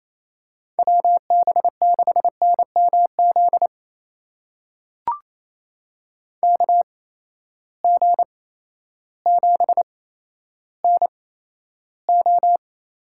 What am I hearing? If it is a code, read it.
Morse code, heard as a keyed tone, about 28 wpm, spaced as for 13 wpm, 700 Hz. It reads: WB6NMZ KG7NO
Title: USA Call Signs - 6 Characters - 8x Character Spacing - 28wpm